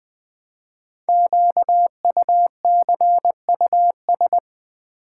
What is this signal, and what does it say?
Morse code, heard as a keyed tone, about 20 wpm, 700 Hz. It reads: QUCUS
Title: Code Groups - 5 Letters - 20wpm